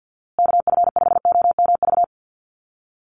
Morse code, heard as a keyed tone, about 50 wpm, 700 Hz. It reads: KF5OMV
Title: USA Call Signs Repeated 2x 50wpm